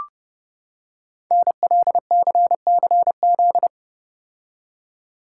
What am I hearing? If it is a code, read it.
Morse code, heard as a keyed tone, about 30 wpm, 700 Hz. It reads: NLCCZ
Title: Code Groups - 5 Letters - 30wpm